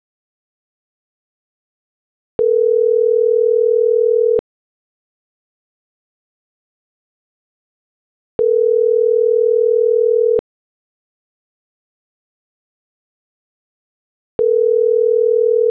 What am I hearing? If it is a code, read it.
Ringback tone (North America), 440 + 480 Hz.